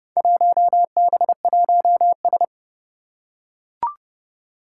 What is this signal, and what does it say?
Morse code, heard as a keyed tone, about 30 wpm, 700 Hz. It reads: 1B1S
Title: Four Challenging Letter-Number Combinations 30wpm